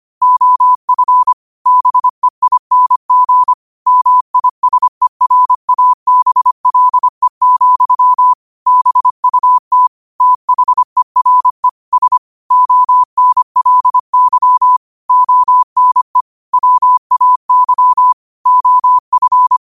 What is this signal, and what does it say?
Morse code, heard as a keyed tone, about 25 wpm, 1000 Hz. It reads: OF BEING MISERABLE, BUT THERE S ONLY ONE WAY OF
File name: qotdfiles / 25wpm